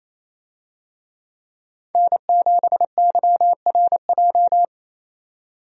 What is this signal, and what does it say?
Morse code, heard as a keyed tone, about 28 wpm, 700 Hz. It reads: N7YRJ
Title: USA Call Signs - 5 Characters - 28wpm